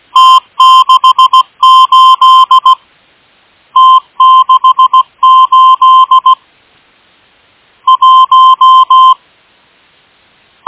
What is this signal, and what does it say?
Morse code, heard as a keyed tone, about 16 wpm, 1040 Hz. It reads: T68 T68 1
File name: M12- -1830utc-8157khz